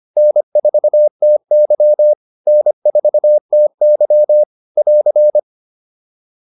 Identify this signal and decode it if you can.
Morse code, heard as a keyed tone, about 25 wpm, 600 Hz. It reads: N4TY N4TY <AR>